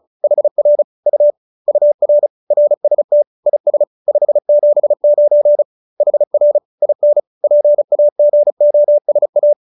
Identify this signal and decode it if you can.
Morse code, heard as a keyed tone, about 34 wpm, 590 Hz. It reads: HR U UR RST IS 579 HR IN PAGOSA
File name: QSO312 35